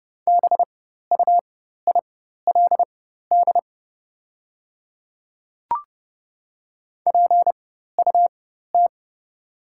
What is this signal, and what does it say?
Morse code, heard as a keyed tone, about 30 wpm, spaced as for 14 wpm, 700 Hz. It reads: BUILD PUT
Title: 4x Character Spacing - Top 200 Words - 30wpm